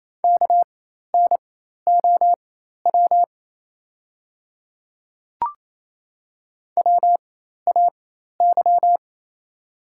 Morse code, heard as a keyed tone, about 28 wpm, spaced as for 13 wpm, 690 Hz. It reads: KNOW WAY
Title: 4x Character Spacing - Top 100 Words - 28wpm